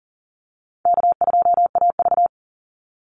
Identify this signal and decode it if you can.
Morse code, heard as a keyed tone, about 40 wpm, 700 Hz. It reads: K2AV
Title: Easy Call Signs Repeated 2x - 40wpm